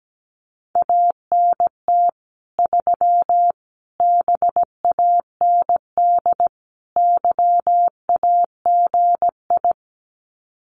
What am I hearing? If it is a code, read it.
Morse code, heard as a keyed tone, about 17 wpm, 700 Hz. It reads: ANT 3 BAND YAGI
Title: Common QSO Phrases 17wpm